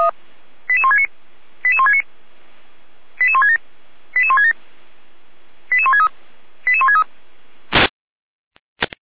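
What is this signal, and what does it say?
Five-tone selective call (ZVEI), heard as quick runs of five tones, tones of about 70 ms. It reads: 88169 88169 88167 88167 88163 88163
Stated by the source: ZVEI five-tone selective call 88391 received by SDR.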